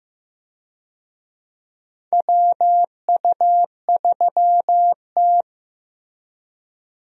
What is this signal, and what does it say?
Morse code, heard as a keyed tone, about 15 wpm, 700 Hz. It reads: WU3T
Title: USA Call Signs - 4 Characters - Repeated 2x - 15wpm